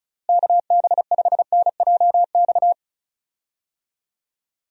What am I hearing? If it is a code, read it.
Morse code, heard as a keyed tone, about 36 wpm, 700 Hz. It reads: KB5NJX
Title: USA Call Signs - 6 Characters - Repeated 2x - 35wpm